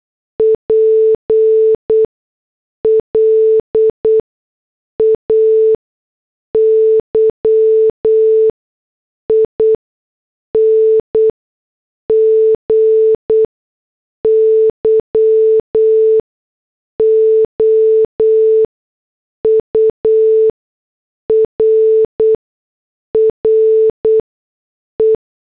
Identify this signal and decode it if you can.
Morse code, heard as a keyed tone, about 8 wpm, spaced as for 6 wpm, 440 Hz. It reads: PLAYING YOUR RE
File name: morse / headlines